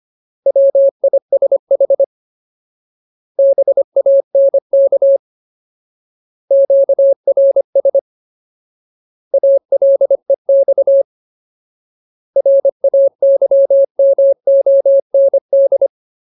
Morse code, heard as a keyed tone, about 25 wpm, 550 Hz. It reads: WISH BANK QRS ALEX RAYMOND
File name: Ultimate Word List / 25WPM Ultimate